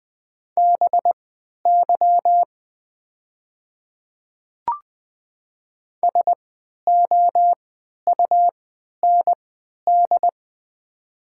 Morse code, heard as a keyed tone, about 20 wpm, spaced as for 11 wpm, 700 Hz. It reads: BY SOUND
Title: 3x Character Spacing - Top 100 Words - 20wpm